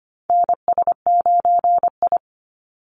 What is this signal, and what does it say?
Morse code, heard as a keyed tone, about 25 wpm, 700 Hz. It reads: NS9I
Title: Call Signs Repeated 2x - 25wpm